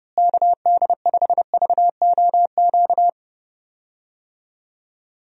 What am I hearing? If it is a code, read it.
Morse code, heard as a keyed tone, about 32 wpm, 700 Hz. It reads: KD5VOQ
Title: USA Call Signs - 6 Characters - Repeated 2x - 30wpm